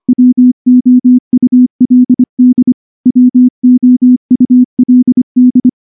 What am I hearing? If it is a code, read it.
Morse code, heard as a keyed tone, about 23 wpm, 250 Hz. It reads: WOULD WOULD
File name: morse / morse-training-25WPM